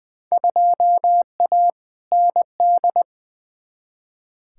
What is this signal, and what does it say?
Morse code, heard as a keyed tone, about 20 wpm, 700 Hz. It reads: 2A ND
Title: Field Day Exchanges Encore - 20wpm